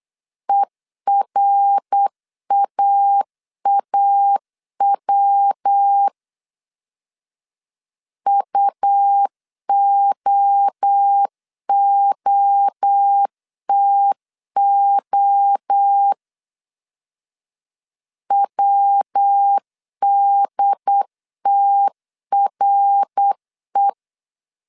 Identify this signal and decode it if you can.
Morse code, heard as a keyed tone, about 8 wpm, 800 Hz. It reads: ERAAW UOOTO WDTRE
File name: CW / DET / 1 3 1500C S B 8wpm 4141